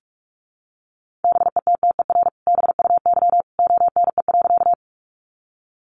Morse code, heard as a keyed tone, about 44 wpm, 700 Hz. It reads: BETTER BUY ONE.
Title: Sentences from Top 700 Words 45wpm